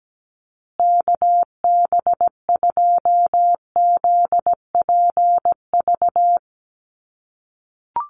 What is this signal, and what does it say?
Morse code, heard as a keyed tone, about 17 wpm, 700 Hz. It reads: KB2ZPV
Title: USA Call Signs Repeated 2x 17wpm